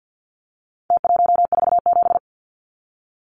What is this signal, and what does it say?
Morse code, heard as a keyed tone, about 50 wpm, 700 Hz. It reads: T147
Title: Numbers - Contest Style - 4-Digit 50wpm